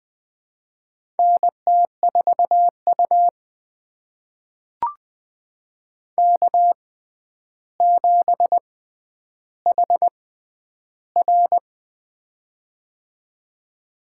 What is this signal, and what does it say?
Morse code, heard as a keyed tone, about 20 wpm, spaced as for 10 wpm, 700 Hz. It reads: NT4U K7HR
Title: USA Call Signs - 4 Characters - 6x Character Spacing - 20wpm